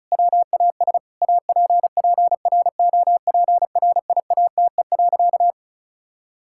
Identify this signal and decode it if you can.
Morse code, heard as a keyed tone, about 36 wpm, 700 Hz. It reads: WAS APPROPRIATE.
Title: Sentences from Top 1000 Words - Review Part 5 - 35wpm